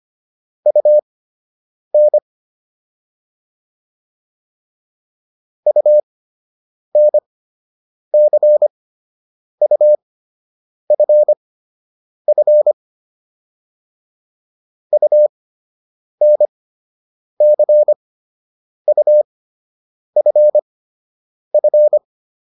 Morse code, heard as a keyed tone, about 25 wpm, spaced as for 8 wpm, 600 Hz. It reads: UN UNCUFF UNCUFF